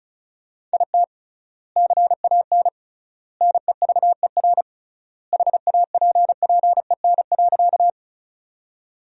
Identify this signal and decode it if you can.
Morse code, heard as a keyed tone, about 34 wpm, 700 Hz. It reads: IT CAN NEVER HAPPEN.
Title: 3x Word-Spacing - Sentences from Top 400 Words - 35wpm